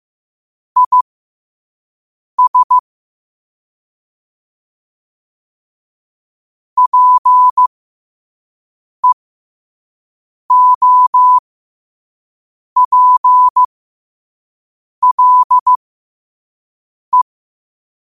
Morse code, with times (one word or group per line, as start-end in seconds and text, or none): 0.8-2.8 IS
6.8-17.2 PEOPLE